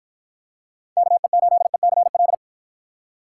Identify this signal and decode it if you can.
Morse code, heard as a keyed tone, about 53 wpm, 700 Hz. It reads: KE8ECL